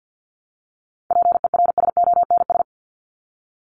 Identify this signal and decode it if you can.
Morse code, heard as a keyed tone, about 50 wpm, 700 Hz. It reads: PERSONS
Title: ES:S Suffix - Top 100 Words with Base Words 50wpm